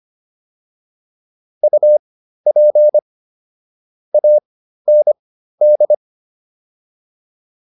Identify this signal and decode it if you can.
Morse code, heard as a keyed tone, about 25 wpm, spaced as for 13 wpm, 600 Hz. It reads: UP AND